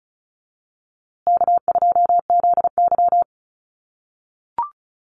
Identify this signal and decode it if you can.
Morse code, heard as a keyed tone, about 34 wpm, 700 Hz. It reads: K2ZY
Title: USA Call Signs Repeated 2x 35wpm